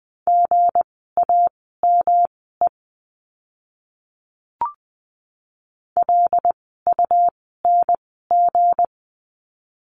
Morse code, heard as a keyed tone, about 20 wpm, spaced as for 15 wpm, 700 Hz. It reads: GAME LUNG
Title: 160 Lesson 26f - Letter g - Words 20wpm